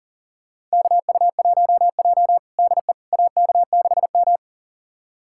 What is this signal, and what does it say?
Morse code, heard as a keyed tone, about 40 wpm, 700 Hz. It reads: KU1J DE AK6M
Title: USA Callsign DE Callsign - 4 Characters - 40wpm